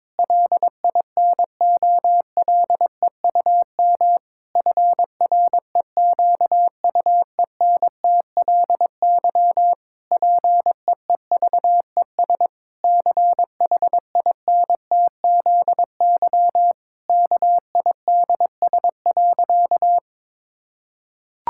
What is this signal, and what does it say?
Morse code, heard as a keyed tone, about 22 wpm, 700 Hz. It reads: LINOLEUM FREQUENTLY PEEVES CHINTZY KIDS.
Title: Pangram Challenge v2 - 22wpm